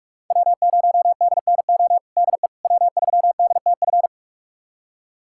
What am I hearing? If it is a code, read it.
Morse code, heard as a keyed tone, about 46 wpm, 700 Hz. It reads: W0DNO DE W3DTF
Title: USA Callsign DE Callsign - 5 Characters - 45wpm